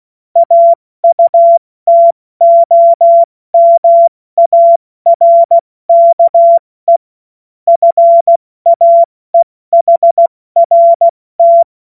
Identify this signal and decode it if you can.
Morse code, heard as a keyed tone, about 16 wpm, 680 Hz. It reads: AUTOMARKE FAEHRT
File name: Morsekurs / Witze / Witz 09 Januar 2024 70 F80